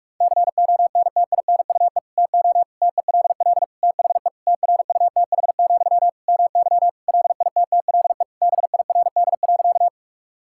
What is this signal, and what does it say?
Morse code, heard as a keyed tone, about 43 wpm, 700 Hz. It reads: CONTINUE TO TELL THE TRUTH, MY LITTLE BIRD.
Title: Sentences from Top 1000 Words - Review Part 6 - 45wpm